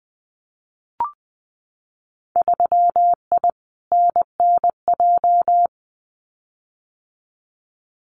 Morse code, heard as a keyed tone, about 20 wpm, 700 Hz. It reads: 3I NNJ
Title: Winter Field Day Exchanges 20wpm